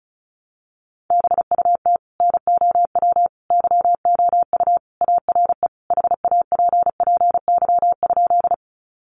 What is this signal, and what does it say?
Morse code, heard as a keyed tone, about 34 wpm, 700 Hz. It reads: BUT NOW YOU ARE HAPPY?
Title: Sentences from Top 1000 Words 35wpm